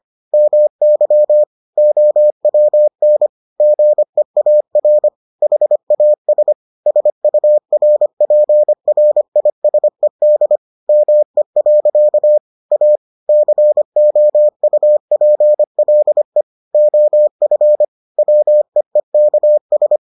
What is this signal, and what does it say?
Morse code, heard as a keyed tone, about 25 wpm, 600 Hz. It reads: MY OWN GEAR HAS SURPRISED ME. A COUPLE OF WEEKS